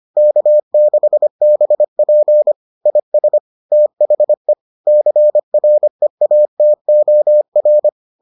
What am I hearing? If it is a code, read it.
Morse code, heard as a keyed tone, about 25 wpm, 600 Hz. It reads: K6BP IS THE CREATOR